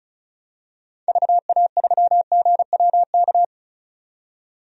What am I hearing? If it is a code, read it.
Morse code, heard as a keyed tone, about 36 wpm, 700 Hz. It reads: VA3GWK
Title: DXCC-100 Call Signs - 6 Characters - 35wpm